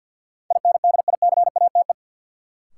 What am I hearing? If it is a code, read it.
Morse code, heard as a keyed tone, about 53 wpm, 700 Hz. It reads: INDICATE
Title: Sentences from Top 1300 Words - New Word Drill Alt - 50wpm